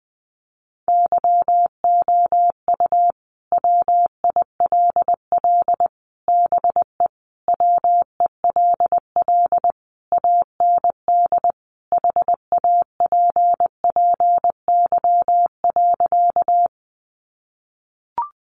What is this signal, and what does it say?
Morse code, heard as a keyed tone, about 20 wpm, 700 Hz. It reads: YOU WILL BE WELL AND HAPPY.
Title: Sentences from Top 1000 Words 20wpm